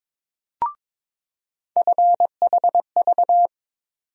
Morse code, heard as a keyed tone, about 22 wpm, 700 Hz. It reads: FHV
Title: USA Call Sign Postfixes - 3 Letters 22wpm